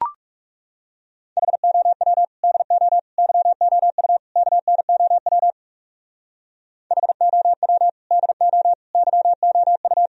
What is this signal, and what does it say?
Morse code, heard as a keyed tone, about 36 wpm, 700 Hz. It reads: HOW DO YOU KNOW HOW DO YOU
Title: Speed-Racing - Sets of 4 Words 30wpm